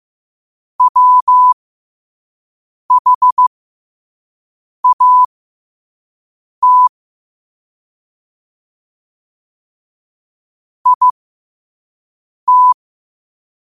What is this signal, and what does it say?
Morse code, heard as a keyed tone, about 15 wpm, spaced as for 5 wpm, 1000 Hz. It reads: WHAT IT
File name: qotdfiles / 5wpm